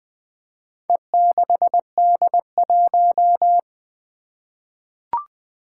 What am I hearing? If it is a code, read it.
Morse code, heard as a keyed tone, about 20 wpm, 700 Hz. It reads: E6D1
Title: Four Challenging Letter-Number Combinations 20wpm